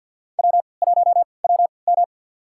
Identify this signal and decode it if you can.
Morse code, heard as a keyed tone, about 50 wpm, spaced as for 28 wpm, 700 Hz. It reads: W1WM
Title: USA Call Signs - 4 Characters - 3x Character Spacing - 50wpm